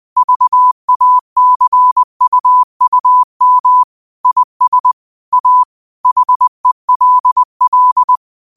Morse code, heard as a keyed tone, about 20 wpm, 1000 Hz. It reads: VACUUM IS A HELL